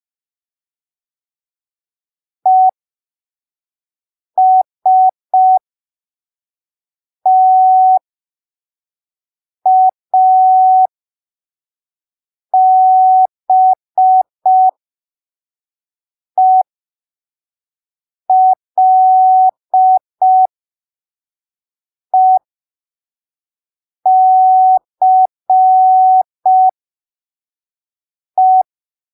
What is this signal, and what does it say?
Morse code, heard as a keyed tone, about 5 wpm, spaced as for 3 wpm, 750 Hz. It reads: ESTABELECE
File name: arquivos / texto / texto 12 05ppm